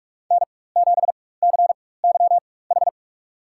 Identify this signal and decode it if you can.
Morse code, heard as a keyed tone, about 43 wpm, spaced as for 21 wpm, 700 Hz. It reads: N7CYH
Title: USA Call Signs - 4x Character Spacing - 45wpm